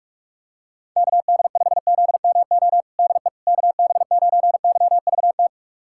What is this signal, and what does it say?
Morse code, heard as a keyed tone, about 42 wpm, 700 Hz. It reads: KD5ZMO DE KB9YVT
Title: USA Callsign DE Callsign - 6 Characters - 45wpm